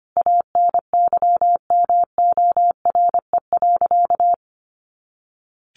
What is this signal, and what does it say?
Morse code, heard as a keyed tone, about 25 wpm, 700 Hz. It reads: ANYMORE.
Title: Easy Phrases 25wpm